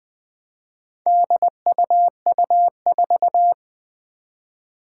Morse code, heard as a keyed tone, about 20 wpm, 700 Hz. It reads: DUU4
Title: Four Challenging Letter-Number Combinations 20wpm